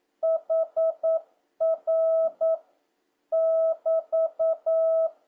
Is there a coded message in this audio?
Yes: Morse code from 0.2 to 5.1 s, heard as a keyed tone, about 9 wpm, 650 Hz. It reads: HR <BT>